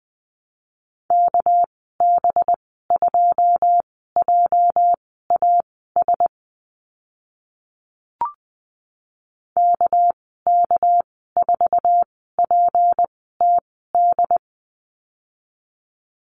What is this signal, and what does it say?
Morse code, heard as a keyed tone, about 20 wpm, spaced as for 14 wpm, 700 Hz. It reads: KB2JAS KK4PTD
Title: 213 Lesson 34g - Number 8 - Call Signs 20wpm